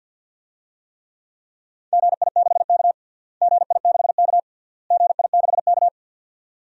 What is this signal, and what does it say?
Morse code, heard as a keyed tone, about 46 wpm, 700 Hz. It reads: GI6K GI6K GI6K
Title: DXCC-100 Call Signs - 4 Characters - Repeated 3x - 50wpm